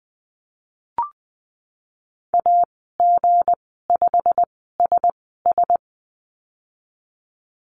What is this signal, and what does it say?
Morse code, heard as a keyed tone, about 20 wpm, spaced as for 15 wpm, 700 Hz. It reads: AG5SS
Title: 167 Lesson 27g - Number 7 - Call Signs 20wpm